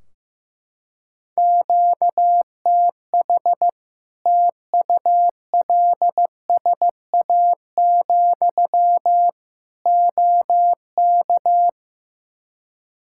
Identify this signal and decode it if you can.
Morse code, heard as a keyed tone, about 15 wpm, 700 Hz. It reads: QTH TULSA, OK
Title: Practice QSO - Exchange 1 - Beginning - 15wpm